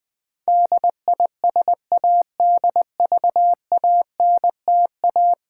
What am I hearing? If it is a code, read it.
Morse code, heard as a keyed tone, about 20 wpm, 700 Hz. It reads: DISADVANTA
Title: Build A Word - 5 Letters At A Time - 12-Letter Words - 20wpm